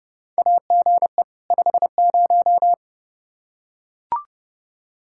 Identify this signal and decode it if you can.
Morse code, heard as a keyed tone, about 32 wpm, 700 Hz. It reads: AGE 50
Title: Practice QSO - Exchange 3 - More Info - 30wpm